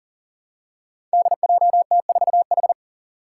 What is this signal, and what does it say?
Morse code, heard as a keyed tone, about 40 wpm, 700 Hz. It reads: DJT4H
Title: Five Challenging Letter-Number Combinations 40wpm